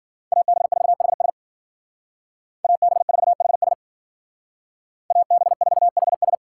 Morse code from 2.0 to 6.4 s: AB4HS AB4HS

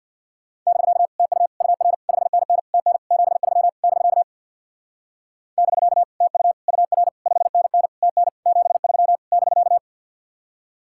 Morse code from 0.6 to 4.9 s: <BK> TU UR 5NN TN 73 <BK>

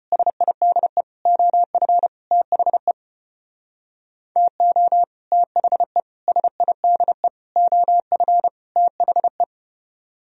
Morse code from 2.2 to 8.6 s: THE TO THE SIDE OF

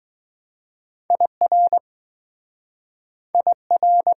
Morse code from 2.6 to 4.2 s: IR

